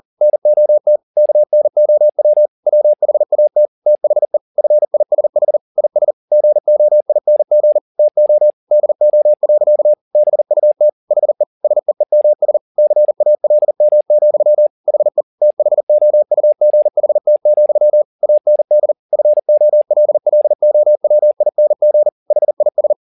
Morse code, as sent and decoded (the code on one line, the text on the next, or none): -. --- - / -.- -. --- .-- / .-- .... .- - / - .... / ..-. .. ... .... / .. ... / --. --- .. -. --. / - --- / -.. --- .-.-.- / -... ..- - / .... / ... . -- ... / -.-. .- .-.. -- --..-- / .... / - .... --- ..- --. .... - --..-- / .- -. -.. / ..-. --- .-.. .-.. --- .-- .. -. --. / .... .. ...
NOT KNOW WHAT THE FISH IS GOING TO DO. BUT HE SEEMS CALM, HE THOUGHT, AND FOLLOWING HIS